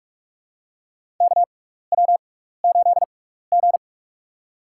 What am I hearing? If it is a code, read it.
Morse code, heard as a keyed tone, about 42 wpm, spaced as for 15 wpm, 700 Hz. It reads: KW8G
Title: USA Call Signs - 4 Characters - 6x Character Spacing - 45wpm